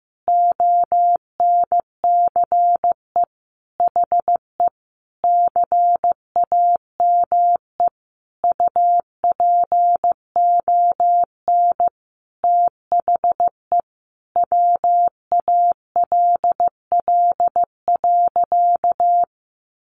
Morse code, heard as a keyed tone, about 14 wpm, 700 Hz. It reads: ONCE HE CAME UPON THE WALL.
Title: Sentences from Top 1100 Words 15wpm